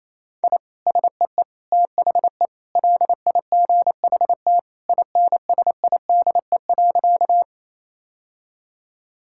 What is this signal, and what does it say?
Morse code, heard as a keyed tone, about 27 wpm, 700 Hz. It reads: I SEE THE LIGHT INSIDE.
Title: Sentences from Top 1000 Words - Review Part 6 - 28wpm